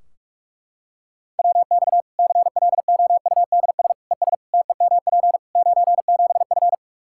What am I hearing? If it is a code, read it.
Morse code, heard as a keyed tone, about 41 wpm, 700 Hz. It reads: WX CLOUDS ES TEMP 97F
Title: Practice QSO - Exchange 2 - Info - 45wpm